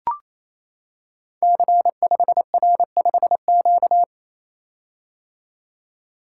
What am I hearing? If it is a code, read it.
Morse code, heard as a keyed tone, about 27 wpm, 700 Hz. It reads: C5R5Q
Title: Code Groups - 5 Letter:Numbers - 28wpm